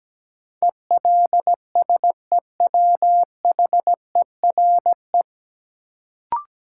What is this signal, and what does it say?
Morse code, heard as a keyed tone, about 17 wpm, 700 Hz. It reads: ELSEWHERE
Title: Sentences from Top 1600 Words - New Word Drill Alt - 17wpm